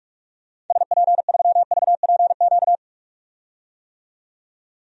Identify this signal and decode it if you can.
Morse code, heard as a keyed tone, about 42 wpm, 700 Hz. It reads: SP3VPQ
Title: DXCC-100 Call Signs - 6 Characters - 45wpm